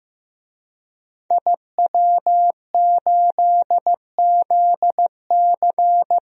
Morse code, heard as a keyed tone, about 15 wpm, 700 Hz. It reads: IW8ZC
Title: DXCC-100 Call Signs - 5 Characters - Repeated 3x - 15wpm